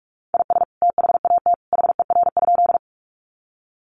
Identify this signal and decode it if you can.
Morse code, heard as a keyed tone, about 44 wpm, 700 Hz. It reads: IS THAT HER?
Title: Sentences from Top 100 Words 45wpm